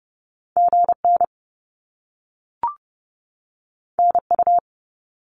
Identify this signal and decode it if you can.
Morse code, heard as a keyed tone, about 30 wpm, 700 Hz. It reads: GN NU
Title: Top Two Letter Combinations in Words 30wpm